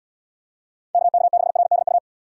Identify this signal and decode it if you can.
Morse code, heard as a keyed tone, about 69 wpm, 700 Hz. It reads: KK6RDU